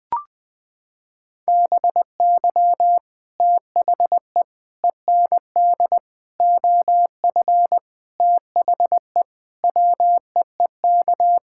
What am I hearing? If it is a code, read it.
Morse code, heard as a keyed tone, about 20 wpm, 700 Hz. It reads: BY THE END OF THE WEEK
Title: Sentences with 5-Word Phrases - 20wpm